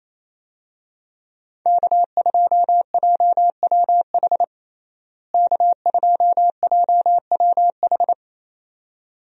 Morse code, heard as a keyed tone, about 28 wpm, 700 Hz. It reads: K2JWH K2JWH
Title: USA Call Signs - 5 Characters - Repeated 2x - 28wpm